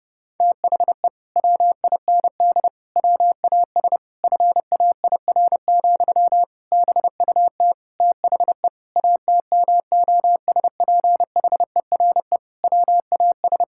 Morse code, heard as a keyed tone, about 32 wpm, 700 Hz. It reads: THE WIND WAS FAIR, BUT THE ATMOSPHERE WAS
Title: Sentences from Top 1600 Words 30wpm